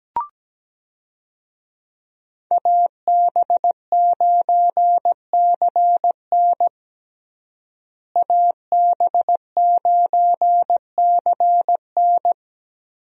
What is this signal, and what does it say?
Morse code, heard as a keyed tone, about 17 wpm, 700 Hz. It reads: AB9CN AB9CN
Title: USA Call Signs - 5 Characters - Repeated 3x - 17wpm